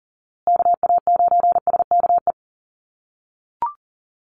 Morse code, heard as a keyed tone, about 40 wpm, 700 Hz. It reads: KA9SKE
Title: USA Call Signs Repeated 2x 40wpm